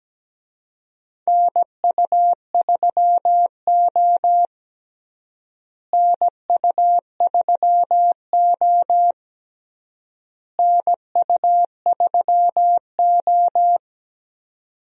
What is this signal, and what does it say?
Morse code, heard as a keyed tone, about 17 wpm, 700 Hz. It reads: NU3O NU3O NU3O
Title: USA Call Signs - 4 Characters - Repeated 3x - 17wpm